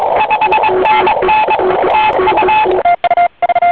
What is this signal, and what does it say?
Morse code, heard as a keyed tone, about 22 wpm, 870 Hz, with radio noise. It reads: 5NN TU